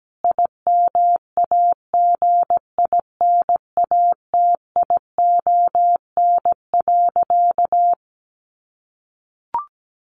Morse code, heard as a keyed tone, about 17 wpm, 700 Hz. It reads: IMAGINATION.
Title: Sentences from Top 1300 Words 17wpm